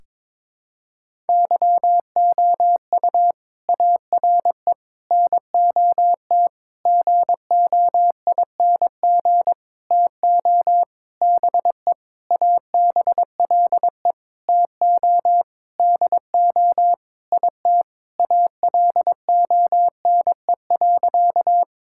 Morse code, heard as a keyed tone, about 22 wpm, 700 Hz. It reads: YOU ARE NOT GOING TO BE ABLE TO DO IT ALONE.
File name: Sentences with 7-Word Phrases - 22wpm